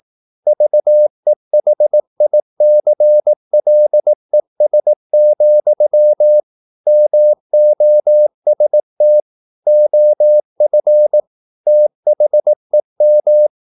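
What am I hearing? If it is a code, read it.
Morse code, heard as a keyed tone, about 18 wpm, 600 Hz. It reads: VEHICLES, MOST OF THEM
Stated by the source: Morse practice at 18 wpm, English text, from CW med Gnister.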